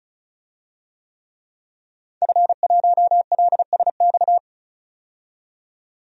Morse code, heard as a keyed tone, about 36 wpm, 700 Hz. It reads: F1LSX
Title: DXCC-100 Call Signs - 5 Characters - 35wpm